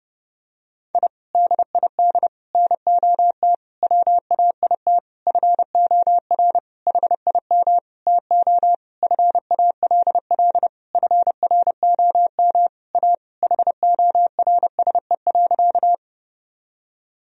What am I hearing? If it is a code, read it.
Morse code, heard as a keyed tone, about 30 wpm, 700 Hz. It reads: I DID NOT WAIT FOR HIM TO FALL FROM A HORSE.
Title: Sentences from Top 1000 Words - Review Part 2 - 30wpm